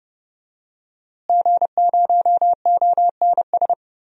Morse code, heard as a keyed tone, about 30 wpm, 700 Hz. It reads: G0ONS
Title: DXCC-100 Call Signs - 5 Characters - Repeated 3x - 30wpm